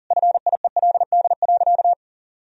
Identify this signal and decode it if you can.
Morse code, heard as a keyed tone, about 37 wpm, 700 Hz. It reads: FIELD.
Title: Sentences from Top 1000 Words - Review Part 4 - 40wpm